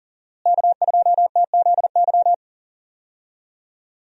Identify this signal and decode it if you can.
Morse code, heard as a keyed tone, about 39 wpm, 700 Hz. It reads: K2TZY
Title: USA Call Signs - 5 Characters - Repeated 3x - 40wpm